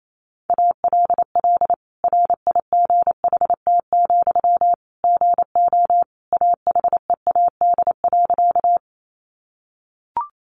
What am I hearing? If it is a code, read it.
Morse code, heard as a keyed tone, about 28 wpm, 700 Hz. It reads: ALL RIGHT, GO AHEAD.
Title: Sentences from Top 1100 Words 28wpm